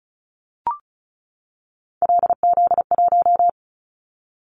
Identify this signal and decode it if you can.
Morse code, heard as a keyed tone, about 34 wpm, 700 Hz. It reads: LZ1
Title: Callsign Prefixes for Top 100 DXCC - 35wpm